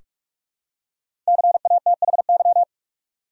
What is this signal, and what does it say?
Morse code, heard as a keyed tone, about 46 wpm, 700 Hz. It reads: CATHY
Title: USA Common Names - Max 5 Letters - 45wpm